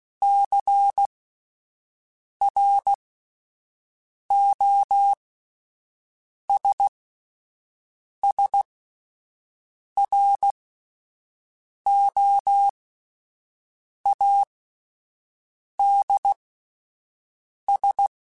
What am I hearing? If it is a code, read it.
Morse code, heard as a keyed tone, about 16 wpm, spaced as for 5 wpm, 800 Hz. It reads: CROSSROADS
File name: fromSMs / FromSMs mp3s / Cities@5